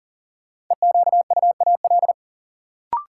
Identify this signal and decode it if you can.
Morse code, heard as a keyed tone, about 37 wpm, 700 Hz. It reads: EQUAL
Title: Sentences from Top 1400 Words - New Word Drill Alt - 40wpm